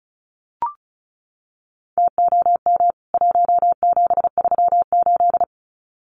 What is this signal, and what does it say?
Morse code, heard as a keyed tone, about 34 wpm, 700 Hz. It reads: TOM 1738
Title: CWT Exchange 35wpm